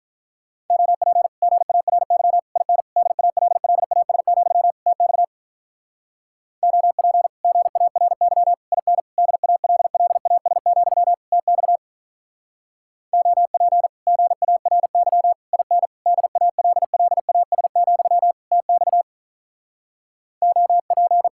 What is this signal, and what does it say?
Morse code, heard as a keyed tone, about 46 wpm, 700 Hz. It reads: OP GARY IN DALLAS, TX OP GARY IN DALLAS, TX OP GARY IN DALLAS, TX OP